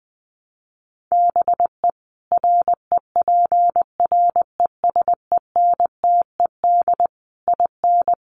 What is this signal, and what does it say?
Morse code, heard as a keyed tone, about 20 wpm, 700 Hz. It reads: BE REPRESENTED IN
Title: Sets of 5 Words 20wpm